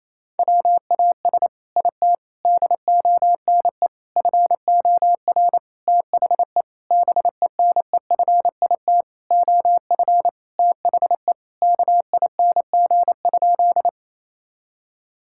Morse code, heard as a keyed tone, about 28 wpm, 700 Hz. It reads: WAS IT DONE FOR THE BENEFIT OF THE KING?
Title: Sentences from Top 1000 Words - Review Part 6 - 28wpm